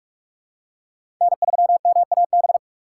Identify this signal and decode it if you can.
Morse code, heard as a keyed tone, about 47 wpm, 700 Hz. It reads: N3MAB
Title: USA Call Signs - 4x Character Spacing - 45wpm